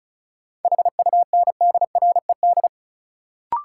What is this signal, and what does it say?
Morse code, heard as a keyed tone, about 36 wpm, 700 Hz. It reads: HUNDRED
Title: Sentences from Top 400 Words - New Word Drill - 35wpm